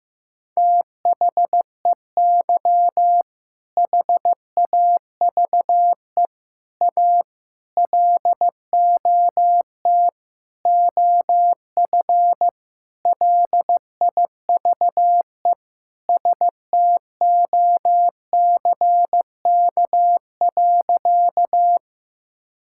Morse code, heard as a keyed tone, about 15 wpm, 700 Hz. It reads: THEY HAVE A LOT OF LIVE STOCK.
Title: Sentences from Top 1000 Words - Review Part 5 - 15wpm